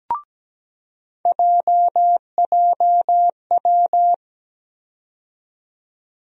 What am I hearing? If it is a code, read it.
Morse code, heard as a keyed tone, about 17 wpm, 700 Hz. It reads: JJW
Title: USA Call Sign Postfixes - 3 Letters 17wpm